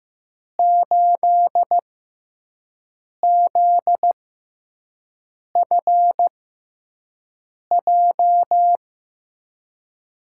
Morse code, heard as a keyed tone, about 15 wpm, spaced as for 5 wpm, 700 Hz. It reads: 8ZFJ
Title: USA Call Signs - 6 Characters - 6x Character Spacing - 15wpm